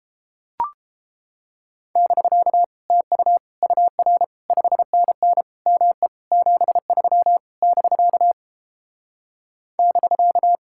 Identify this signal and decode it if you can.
Morse code, heard as a keyed tone, about 29 wpm, 700 Hz. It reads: <BK> TU UR 5NN ME 73 <BK> <BK>